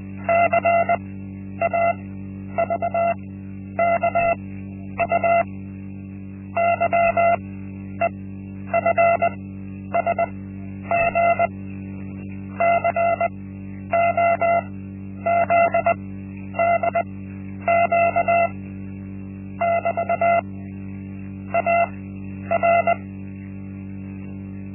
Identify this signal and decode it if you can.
Morse code, heard as a keyed tone, about 20 wpm, spaced as for 10 wpm, 680 Hz. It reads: CAVKU YEFSG COZDQ <BT> AR
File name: Linksmakalnis-1